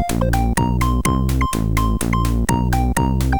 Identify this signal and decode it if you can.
There is no code.